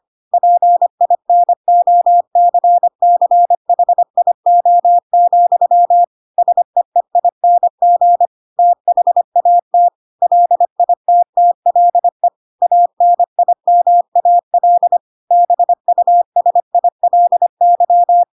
Morse code, heard as a keyed tone, about 25 wpm, 700 Hz. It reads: PINOCCHIO, SEEING THAT LITTLE ANIMAL BUSILY
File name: Pinocchio / Pinocchio ch7 25 25 ik1hge